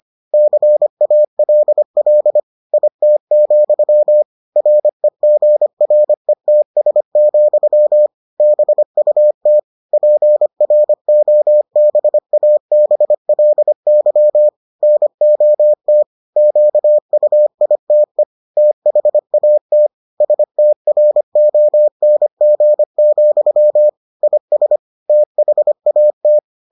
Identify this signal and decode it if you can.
Morse code, heard as a keyed tone, about 25 wpm, 600 Hz. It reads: CALL IT, REGRETS, BUT PROBABLY NOT QUITE THAT STRONG, IS THAT